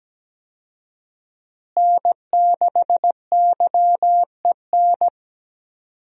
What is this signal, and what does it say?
Morse code, heard as a keyed tone, about 17 wpm, 700 Hz. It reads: N6YEN
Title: USA Call Signs - 5 Characters - Repeated 3x - 17wpm